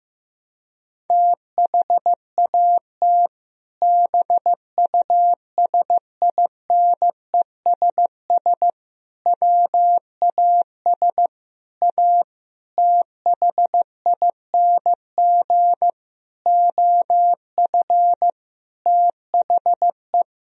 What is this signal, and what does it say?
Morse code, heard as a keyed tone, about 15 wpm, 700 Hz. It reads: THAT BUSINESS WAS A THING OF THE
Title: Sentences from Top 1000 Words - Review Part 3 - 15wpm